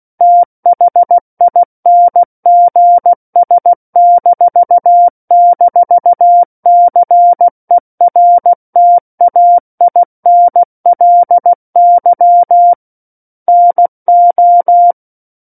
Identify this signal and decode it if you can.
Morse code, heard as a keyed tone, about 16 wpm, 700 Hz. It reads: THINGS--CERTAINLY NO
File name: Book One / 07-Chapter 7 - 16 WPM